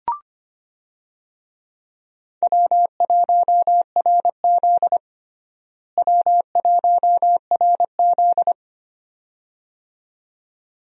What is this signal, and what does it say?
Morse code, heard as a keyed tone, about 25 wpm, 700 Hz. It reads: W1RZ W1RZ
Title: USA Call Signs - 4 Characters - Repeated 2x - 25wpm